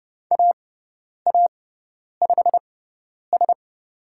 Morse code, heard as a keyed tone, about 30 wpm, spaced as for 10 wpm, 700 Hz. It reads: AA5S
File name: Lesson 18g - Letter m - Call Signs 30wpm